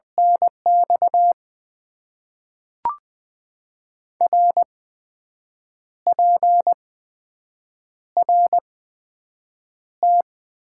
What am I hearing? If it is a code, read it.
Morse code, heard as a keyed tone, about 20 wpm, 700 Hz. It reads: N X R P R T
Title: Top 90 QSO Elements - 8x Character Spacing - 20wpm